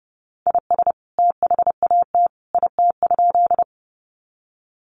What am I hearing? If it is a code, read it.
Morse code, heard as a keyed tone, about 30 wpm, 700 Hz. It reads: IS THAT IT?
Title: Sentences from Top 100 Words 30wpm